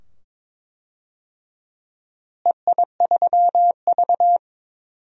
Morse code, heard as a keyed tone, about 22 wpm, 700 Hz. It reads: EI3V
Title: DXCC-100 Call Signs - 4 Characters - Repeated 2x - 22wpm